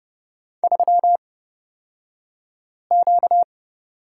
Morse code, heard as a keyed tone, about 30 wpm, spaced as for 5 wpm, 700 Hz. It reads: 3Q
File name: Lesson 38d - Letter q - Two Characters 30wpm